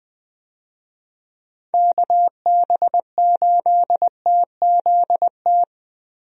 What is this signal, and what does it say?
Morse code, heard as a keyed tone, about 20 wpm, 700 Hz. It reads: KB8TZT
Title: USA Call Signs - 6 Characters - 20wpm